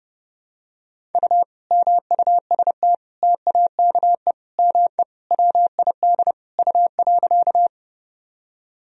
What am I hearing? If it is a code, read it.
Morse code, heard as a keyed tone, about 30 wpm, 700 Hz. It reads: U MUST TAKE ME WID U.
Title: Sentences from Top 200 Words and Abbreviations 30wpm